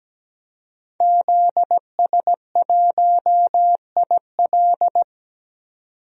Morse code, heard as a keyed tone, about 17 wpm, 700 Hz. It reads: ZS1IL
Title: DXCC-100 Call Signs - 5 Characters - 17wpm